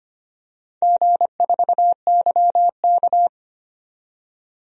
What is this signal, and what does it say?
Morse code, heard as a keyed tone, about 25 wpm, 700 Hz. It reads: G4YK
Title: DXCC-100 Call Signs - 4 Characters - 25wpm